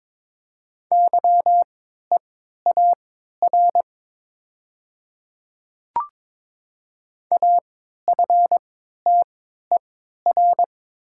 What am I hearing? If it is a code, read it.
Morse code, heard as a keyed tone, about 22 wpm, spaced as for 13 wpm, 690 Hz. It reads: YEAR AFTER
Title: 3x Character Spacing - Top 200 Words - 22wpm